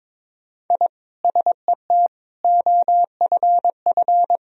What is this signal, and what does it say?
Morse code, heard as a keyed tone, about 22 wpm, 700 Hz. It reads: I SET OFF
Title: Sentences from Top 1000 Words - Review Part 1 - 22wpm